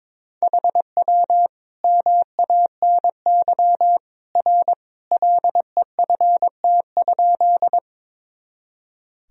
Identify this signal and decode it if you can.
Morse code, heard as a keyed tone, about 22 wpm, 700 Hz. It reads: HW MANY R LEFT?
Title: Sentences from Top 300 Words and Abbreviations 22wpm